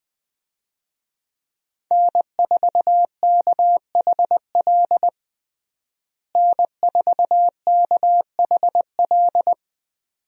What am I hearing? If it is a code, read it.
Morse code, heard as a keyed tone, about 20 wpm, 700 Hz. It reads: N4KHL N4KHL
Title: USA Call Signs - 5 Characters - Repeated 2x - 20wpm